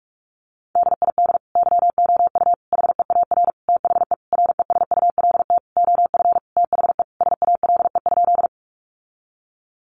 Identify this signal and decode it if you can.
Morse code, heard as a keyed tone, about 44 wpm, 700 Hz. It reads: DID YOU HEAR THE RESULT OF THE SALE?
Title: Sentences from Top 1400 Words 45wpm